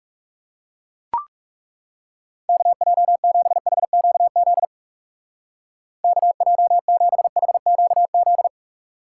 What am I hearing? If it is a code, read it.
Morse code, heard as a keyed tone, about 36 wpm, 700 Hz. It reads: KJ7HQZ KJ7HQZ
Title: Speed Racing - USA 2x3 Call Signs 30wpm